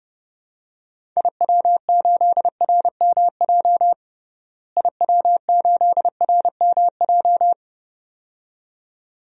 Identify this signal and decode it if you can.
Morse code, heard as a keyed tone, about 30 wpm, 700 Hz. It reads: IW8RMJ IW8RMJ
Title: DXCC-100 Call Signs - 6 Characters - Repeated 2x - 30wpm